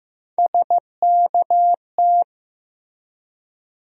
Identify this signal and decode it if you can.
Morse code, heard as a keyed tone, about 15 wpm, 700 Hz. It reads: SKT
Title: USA Call Sign Postfixes - 3 Letters 15wpm